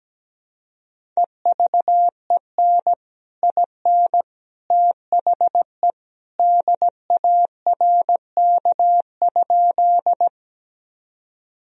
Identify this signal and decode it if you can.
Morse code, heard as a keyed tone, about 17 wpm, 700 Hz. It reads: EVEN IN THE DARK?
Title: Sentences from Top 1000 Words - Review Part 1 - 17wpm